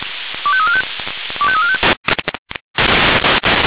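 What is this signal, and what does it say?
Five-tone selective call (ZVEI), heard as quick runs of five tones, tones of about 70 ms. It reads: 25345 25345
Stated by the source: ZVEI five-tone selective call received on an SDR.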